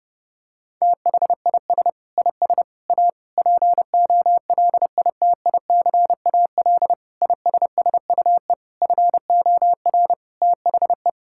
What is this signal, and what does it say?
Morse code, heard as a keyed tone, about 30 wpm, 700 Hz. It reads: THIS IS A POLITICAL ISSUE FOR THE
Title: Sentences from Top 1000 Words - Review Part 6 - 30wpm